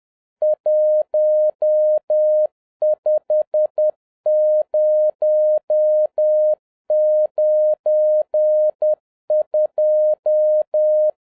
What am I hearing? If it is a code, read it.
Morse code, heard as a keyed tone, about 10 wpm, 610 Hz. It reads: 15092